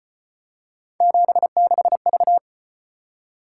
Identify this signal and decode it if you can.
Morse code, heard as a keyed tone, about 33 wpm, 700 Hz. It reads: 76V